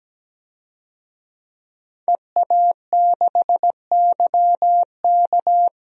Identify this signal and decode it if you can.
Morse code, heard as a keyed tone, about 17 wpm, 700 Hz. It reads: EA6YK
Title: DXCC-100 Call Signs - 5 Characters - Repeated 3x - 17wpm